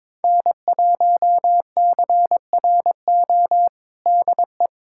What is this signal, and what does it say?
Morse code, heard as a keyed tone, about 21 wpm, 700 Hz. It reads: N1CRO DE
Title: USA Callsign DE Callsign - 5 Characters - 22wpm